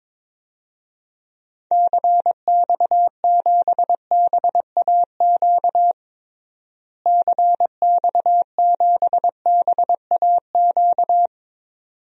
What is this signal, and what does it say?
Morse code, heard as a keyed tone, about 22 wpm, 700 Hz. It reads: CX7BAQ CX7BAQ
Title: DXCC-100 Call Signs - 6 Characters - Repeated 2x - 22wpm